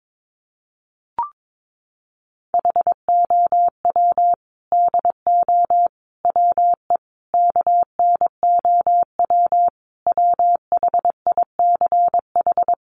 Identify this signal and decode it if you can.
Morse code, heard as a keyed tone, about 22 wpm, 700 Hz. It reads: HOW DO WE KNOW WHICH
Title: A Sound Of Thunder 22wpm